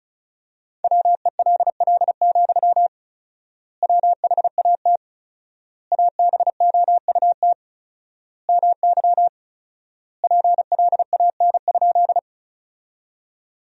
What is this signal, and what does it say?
Morse code, heard as a keyed tone, about 34 wpm, 700 Hz. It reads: WELL, WHAT ABOUT MY PLAN?
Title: 4x Word-Spacing - Sentences from Top 500 Words - 35wpm